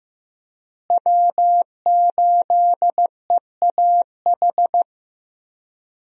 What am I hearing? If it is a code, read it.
Morse code, heard as a keyed tone, about 15 wpm, 700 Hz. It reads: W8EAH
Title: USA Call Signs - 5 Characters - Repeated 2x - 15wpm